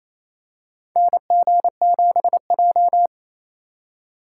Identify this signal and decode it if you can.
Morse code, heard as a keyed tone, about 28 wpm, 700 Hz. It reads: NG7J
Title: USA Call Signs - 4 Characters - 3x Character Spacing - 28wpm